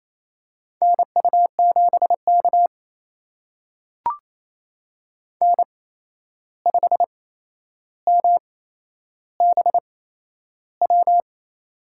Morse code, heard as a keyed tone, about 28 wpm, spaced as for 11 wpm, 700 Hz. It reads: NU7K N5MBW
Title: USA Call Signs - 8x Character Spacing - 28wpm